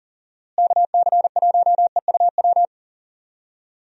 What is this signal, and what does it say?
Morse code, heard as a keyed tone, about 40 wpm, 700 Hz. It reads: KC1EUW
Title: USA Call Signs - 6 Characters - Repeated 3x - 40wpm